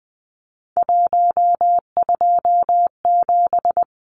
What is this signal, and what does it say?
Morse code, heard as a keyed tone, about 20 wpm, 700 Hz. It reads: 127
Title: Numbers - 3-Digit 20wpm